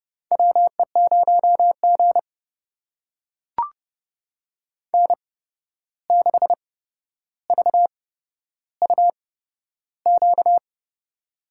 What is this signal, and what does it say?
Morse code, heard as a keyed tone, about 30 wpm, spaced as for 12 wpm, 700 Hz. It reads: WE0G N6VUQ
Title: USA Call Signs - 8x Character Spacing - 30wpm